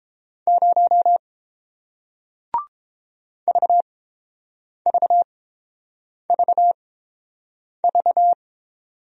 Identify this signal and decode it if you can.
Morse code, heard as a keyed tone, about 31 wpm, spaced as for 7 wpm, 700 Hz. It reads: 0 VVVV